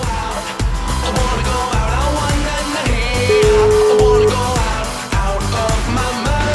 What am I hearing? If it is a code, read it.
Ringback tone (Europe, 425 Hz), with noise.